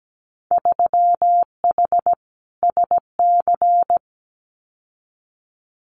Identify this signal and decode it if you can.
Morse code, heard as a keyed tone, about 17 wpm, 700 Hz. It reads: 3H SC
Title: Winter Field Day Exchanges 17wpm